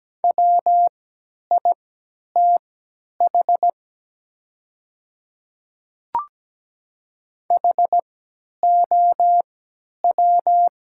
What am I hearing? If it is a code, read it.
Morse code, heard as a keyed tone, about 17 wpm, spaced as for 10 wpm, 690 Hz. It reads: WITH HOW
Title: 3x Character Spacing - Top 100 Words - 17wpm